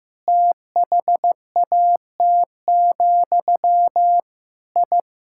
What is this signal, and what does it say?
Morse code, heard as a keyed tone, about 15 wpm, 700 Hz. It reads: THAT, I
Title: Sentences from Top 1000 Words - Review Part 6 - 15wpm